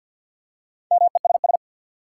Morse code, heard as a keyed tone, about 47 wpm, 700 Hz. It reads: MESS
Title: Four Letter Words - 50wpm